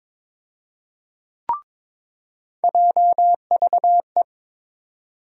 Morse code, heard as a keyed tone, about 22 wpm, 700 Hz. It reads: JVE
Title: USA Call Sign Postfixes - 3 Letters 22wpm